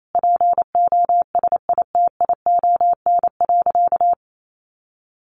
Morse code, heard as a keyed tone, about 28 wpm, 700 Hz. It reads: POSITION.
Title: Sentences from Top 1100 Words 28wpm